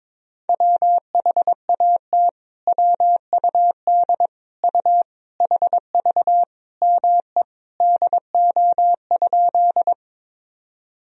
Morse code, heard as a keyed tone, about 22 wpm, 700 Hz. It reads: WHAT WUD U HV ME DO?
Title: Sentences from Top 200 Words and Abbreviations 22wpm